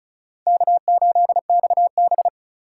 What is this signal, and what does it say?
Morse code, heard as a keyed tone, about 46 wpm, 700 Hz. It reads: K8XB